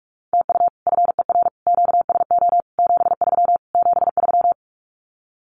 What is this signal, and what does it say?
Morse code, heard as a keyed tone, about 44 wpm, 700 Hz. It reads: TU FER QSO 73 73
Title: Common QSO Phrases Encore 45wpm